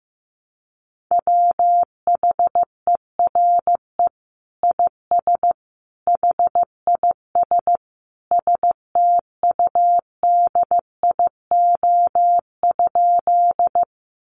Morse code, heard as a keyed tone, about 15 wpm, 700 Hz. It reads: WHERE IS HIS STUDIO?
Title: Sentences from Top 1200 Words 15wpm